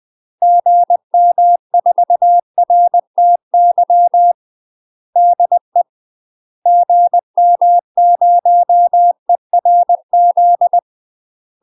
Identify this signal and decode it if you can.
Morse code, heard as a keyed tone, about 20 wpm, 700 Hz. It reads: GM4RTY DE GM0ERZ
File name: UK MASTER QSOs TRAINING / 20WPM / UK MASTER QSOs 0066